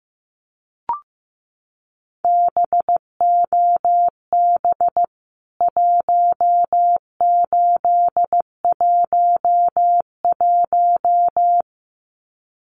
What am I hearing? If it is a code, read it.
Morse code, heard as a keyed tone, about 15 wpm, 700 Hz. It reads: BOB 1811
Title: CWT Exchange 15wpm